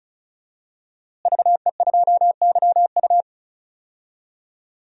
Morse code, heard as a keyed tone, about 34 wpm, 700 Hz. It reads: VE2YU
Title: DXCC-100 Call Signs - 5 Characters - Repeated 2x - 35wpm